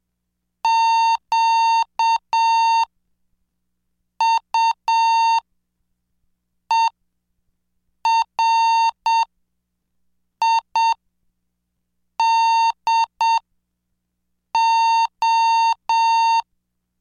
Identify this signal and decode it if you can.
Morse code, heard as a keyed tone, about 7 wpm, spaced as for 5 wpm, 910 Hz. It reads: QUERIDO
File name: cursos / CDestudo / 20 Track 20